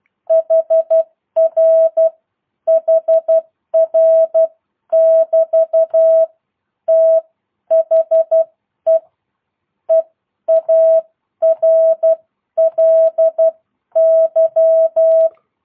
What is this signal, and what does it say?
Morse code, heard as a keyed tone, about 12 wpm, spaced as for 10 wpm, 650 Hz. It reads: HR HR <BT> THE EARLY